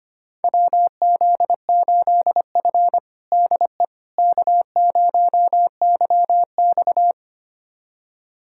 Morse code, heard as a keyed tone, about 25 wpm, 700 Hz. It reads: WZ8F DE K0YX
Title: USA Callsign DE Callsign - 4 Characters - 25wpm